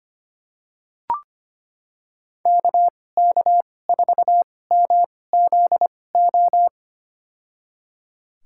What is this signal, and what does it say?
Morse code, heard as a keyed tone, about 25 wpm, spaced as for 18 wpm, 700 Hz. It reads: KK4MZO
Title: USA Call Signs - 6 Characters - 2x Character Spacing - 25wpm